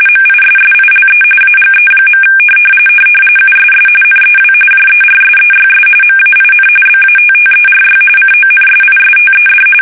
RTTY, about 45.45 baud. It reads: MISSILE AGE. USS NEW JERSEYS MAIN BATTERY CONSISTED
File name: fsk850 / battleshipnj